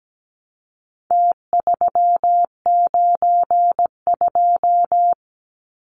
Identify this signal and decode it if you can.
Morse code, heard as a keyed tone, about 17 wpm, 700 Hz. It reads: T392
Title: Numbers - Contest Style - 4-Digit 17wpm